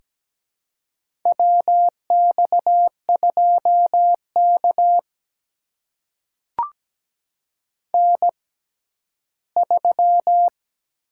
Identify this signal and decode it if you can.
Morse code, heard as a keyed tone, about 17 wpm, spaced as for 12 wpm, 700 Hz. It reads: WX2K N3
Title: USA Call Signs - 6x Character Spacing - 17wpm